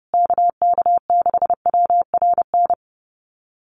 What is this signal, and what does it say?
Morse code, heard as a keyed tone, about 30 wpm, 700 Hz. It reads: KK6WRN
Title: USA Call Signs Repeated 2x 30wpm